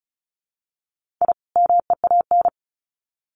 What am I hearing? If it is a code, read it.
Morse code, heard as a keyed tone, about 34 wpm, 700 Hz. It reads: I MEAN
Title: Sets of 2 Words 35wpm